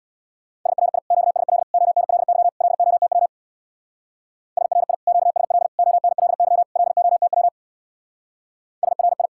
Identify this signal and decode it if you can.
Morse code, heard as a keyed tone, about 60 wpm, 700 Hz. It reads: SRI QSL ONLY LOTW SRI QSL ONLY LOTW SRI
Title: Speed-Racing Common QSO Phrases Encore - 50wpm